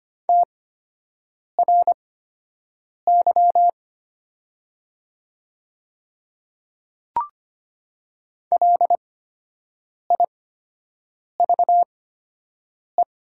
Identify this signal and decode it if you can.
Morse code, heard as a keyed tone, about 25 wpm, spaced as for 7 wpm, 700 Hz. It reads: TRY LIVE
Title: 8x Character Spacing - Top 200 Words - 25wpm